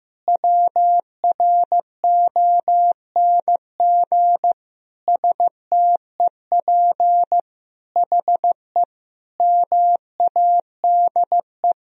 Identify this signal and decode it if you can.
Morse code, heard as a keyed tone, about 15 wpm, 700 Hz. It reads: WRONG STEP HE MADE
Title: Sentences from Top 1000 Words - Review Part 2 - 15wpm